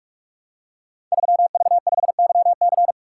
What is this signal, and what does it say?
Morse code, heard as a keyed tone, about 44 wpm, 700 Hz. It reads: 3V5YC
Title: Code Groups - 5 Letter:Numbers - 45wpm